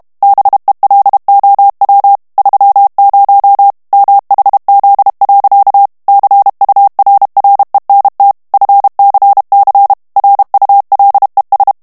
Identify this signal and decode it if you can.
Morse code, heard as a keyed tone, about 32 wpm, 800 Hz. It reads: BELOW 30 MHZ. CURRENT FCC RULES
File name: ANGLAIS texte clair 3 vitesse 30 mots minute